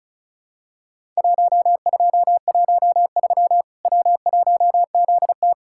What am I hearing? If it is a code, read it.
Morse code, heard as a keyed tone, about 34 wpm, 700 Hz. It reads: 1213 W1ZT
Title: Worked All Europe DX Contest - QTC Exchange - 35wpm